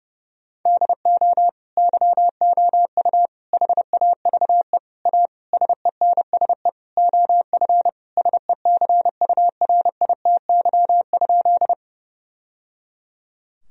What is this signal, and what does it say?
Morse code, heard as a keyed tone, about 32 wpm, 700 Hz. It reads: DO YOU HAVE A SENSE OF SECURITY?
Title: Sentences from Top 1000 Words - Review Part 4 - 30wpm